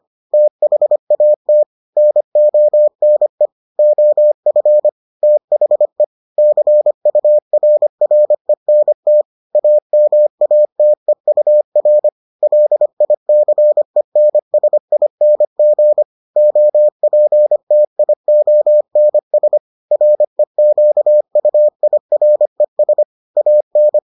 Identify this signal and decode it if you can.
Morse code, heard as a keyed tone, about 25 wpm, 600 Hz. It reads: THAT NONE OF THE CURRENT AMATEUR LICENSING OPTIONS REQUIRES AN